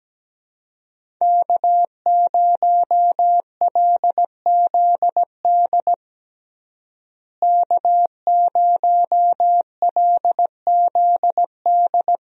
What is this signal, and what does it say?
Morse code, heard as a keyed tone, about 17 wpm, 700 Hz. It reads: K0LZD K0LZD
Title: USA Call Signs - 5 Characters - Repeated 3x - 17wpm